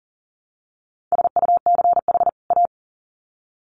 Morse code, heard as a keyed tone, about 40 wpm, 700 Hz. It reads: SUCH A
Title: Sets of 2 Words 40wpm